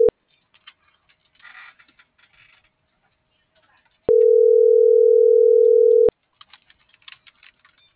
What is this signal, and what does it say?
Ringback tone (North America), 440 + 480 Hz.